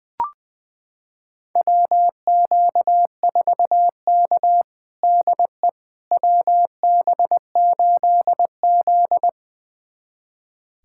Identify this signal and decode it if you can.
Morse code, heard as a keyed tone, about 21 wpm, 700 Hz. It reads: WQ4K DE WB8Z
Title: USA Callsign DE Callsign - 4 Characters - 20wpm